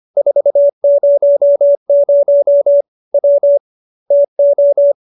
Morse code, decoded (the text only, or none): 400 W TO